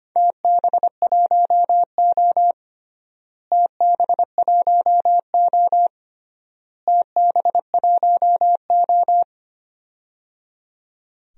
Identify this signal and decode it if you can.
Morse code, heard as a keyed tone, about 25 wpm, 700 Hz. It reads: TB1O TB1O TB1O
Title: DXCC-100 Call Signs - 4 Characters - Repeated 3x - 25wpm